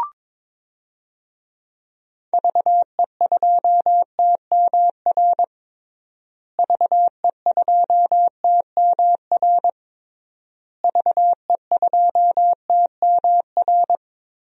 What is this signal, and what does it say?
Morse code, heard as a keyed tone, about 22 wpm, 700 Hz. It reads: VE2TMR VE2TMR VE2TMR
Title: DXCC-100 Call Signs - 6 Characters - Repeated 3x - 22wpm